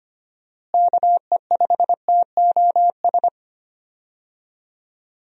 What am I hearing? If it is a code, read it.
Morse code, heard as a keyed tone, about 25 wpm, 700 Hz. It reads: KE5TOS
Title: USA Call Signs - 6 Characters - Repeated 2x - 25wpm